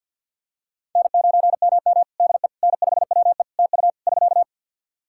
Morse code, heard as a keyed tone, about 49 wpm, 700 Hz. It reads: N9MM DE N5WE TU <SK>
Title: Practice QSO - Exchange 4 - Ending - 50wpm